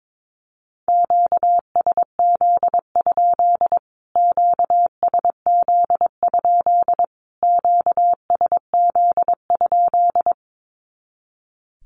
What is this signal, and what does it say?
Morse code, heard as a keyed tone, about 22 wpm, 700 Hz. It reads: QSZ? QSZ? QSZ?
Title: Q-Codes Repeated 3x 22wpm